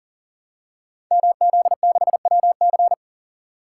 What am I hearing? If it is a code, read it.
Morse code, heard as a keyed tone, about 42 wpm, 700 Hz. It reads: MZ6WC
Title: Code Groups - 5 Letter:Numbers - 40wpm